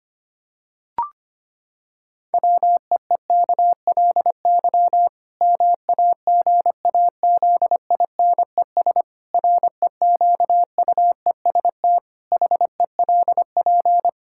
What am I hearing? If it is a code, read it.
Morse code, heard as a keyed tone, about 25 wpm, 700 Hz. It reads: WEEKLY MAGAZINES REQUEST HELP
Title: Pangram Challenge v2 - 25wpm